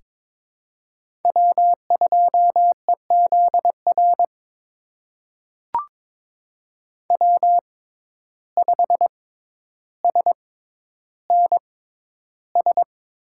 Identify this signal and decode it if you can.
Morse code, heard as a keyed tone, about 22 wpm, spaced as for 11 wpm, 700 Hz. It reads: W2EZR W5SNS
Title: USA Call Signs - 5 Characters - 6x Character Spacing - 22wpm